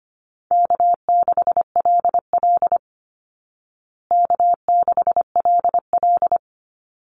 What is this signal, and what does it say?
Morse code, heard as a keyed tone, about 25 wpm, 700 Hz. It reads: K6LL K6LL
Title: Easy Call Signs Repeated 2x - 25wpm